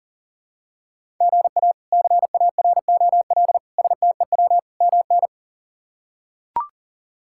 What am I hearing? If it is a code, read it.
Morse code, heard as a keyed tone, about 37 wpm, 700 Hz. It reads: GA CAROL STEW MN
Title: K1USN SST QSO Practice - 40wpm